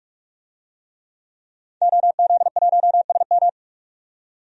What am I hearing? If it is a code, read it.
Morse code, heard as a keyed tone, about 44 wpm, 700 Hz. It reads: OZ1SM
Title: DXCC-100 Call Signs - 5 Characters - 45wpm